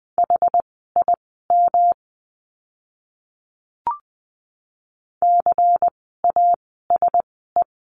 Morse code, heard as a keyed tone, about 20 wpm, spaced as for 14 wpm, 700 Hz. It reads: HIM CASE
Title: 244 Lesson 39f - Letter z - Words 20wpm